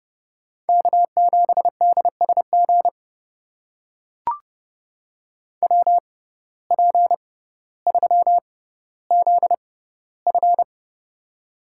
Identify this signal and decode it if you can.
Morse code, heard as a keyed tone, about 30 wpm, spaced as for 15 wpm, 700 Hz. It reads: K7DSG WP3ZF
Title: USA Call Signs - 5 Characters - 6x Character Spacing - 30wpm